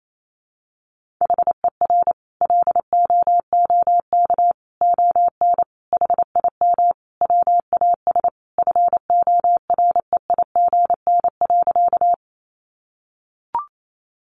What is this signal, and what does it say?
Morse code, heard as a keyed tone, about 28 wpm, 700 Hz. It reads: HER LOOK ON HIM WAS FOREIGN.
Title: Sentences from Top 1400 Words 28wpm